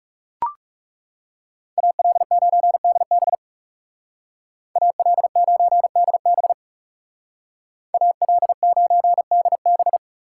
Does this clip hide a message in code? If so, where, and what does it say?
Yes: Morse code from 1.8 to 10.0 s, heard as a keyed tone, about 39 wpm, 700 Hz. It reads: AL9DB AL9DB AL9DB